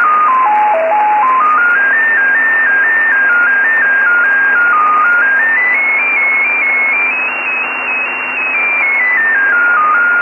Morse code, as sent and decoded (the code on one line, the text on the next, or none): none